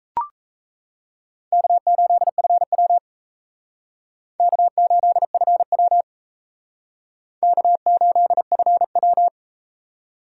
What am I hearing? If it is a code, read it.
Morse code, heard as a keyed tone, about 37 wpm, 700 Hz. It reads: K8FW K8FW K8FW